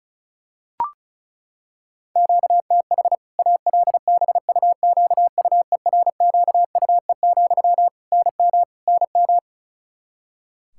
Morse code, heard as a keyed tone, about 34 wpm, 700 Hz. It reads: QTH ALBUQUERQUE, NM NM
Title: Practice QSO - Exchange 1 - Beginning - 35wpm